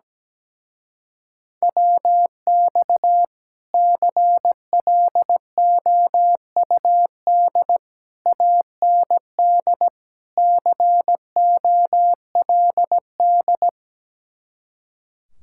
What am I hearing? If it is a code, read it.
Morse code, heard as a keyed tone, about 17 wpm, 700 Hz. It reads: WX CLOUD AND COLD
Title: Speed-Racing Common QSO Phrases Encore - 17wpm